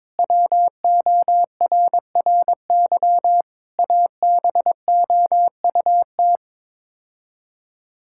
Morse code, heard as a keyed tone, about 22 wpm, 700 Hz. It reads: WORRY ABOUT
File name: practice / 22wpm / Build A Phrase - 5-Word Phrases - 22wpm